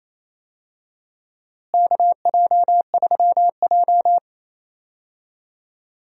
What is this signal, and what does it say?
Morse code, heard as a keyed tone, about 28 wpm, 700 Hz. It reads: KJ3J
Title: USA Call Signs - 4 Characters - Repeated 3x - 28wpm